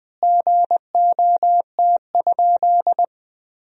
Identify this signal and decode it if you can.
Morse code, heard as a keyed tone, about 20 wpm, 700 Hz. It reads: GOT?
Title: 4x Word-Spacing - Sentences from Top 300 Words - 20wpm